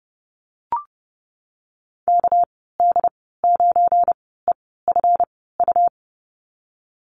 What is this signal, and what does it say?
Morse code, heard as a keyed tone, about 32 wpm, spaced as for 17 wpm, 700 Hz. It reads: KD9EFU
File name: Lesson 37g - Letter x - Call Signs 30wpm